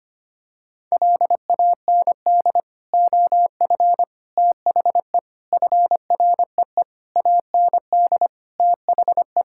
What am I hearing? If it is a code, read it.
Morse code, heard as a keyed tone, about 25 wpm, 700 Hz. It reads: LAND OF THE FREE AND THE
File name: practice / 25wpm / Sentences with 9-Word Phrases - 25wpm